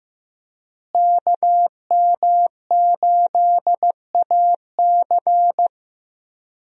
Morse code, heard as a keyed tone, about 15 wpm, 700 Hz. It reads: KM8AC
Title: USA Call Signs - 5 Characters - Repeated 3x - 15wpm